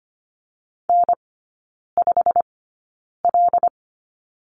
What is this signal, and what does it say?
Morse code, heard as a keyed tone, about 25 wpm, spaced as for 9 wpm, 700 Hz. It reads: N5L
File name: Lesson 15e - Number 5 - Three Characters 25wpm